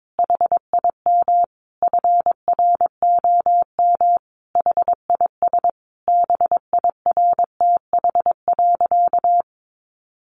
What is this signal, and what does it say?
Morse code, heard as a keyed tone, about 22 wpm, 700 Hz. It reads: HIM FROM HIS BIRTH.
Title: Sentences from Top 900 Words 22wpm